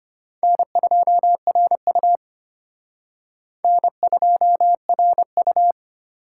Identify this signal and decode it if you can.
Morse code, heard as a keyed tone, about 32 wpm, 700 Hz. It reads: N2RU N2RU